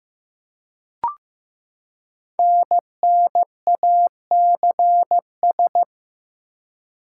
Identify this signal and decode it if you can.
Morse code, heard as a keyed tone, about 15 wpm, 700 Hz. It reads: NNACS
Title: Code Groups - 5 Letters - 15wpm